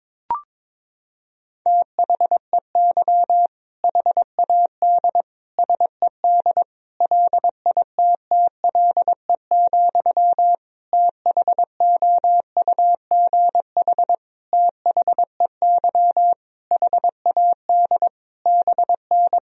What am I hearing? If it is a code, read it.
Morse code, heard as a keyed tone, about 22 wpm, 700 Hz. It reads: THEY HAD SED LITTLE, THOUGH THEY HAD BN
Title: Sentences from Top 400 Words and Abbreviations 22wpm